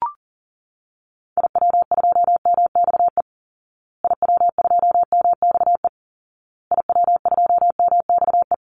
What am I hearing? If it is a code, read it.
Morse code, heard as a keyed tone, about 40 wpm, 700 Hz. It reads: IW2MXE IW2MXE IW2MXE
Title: Call Signs Repeated 2x - 40wpm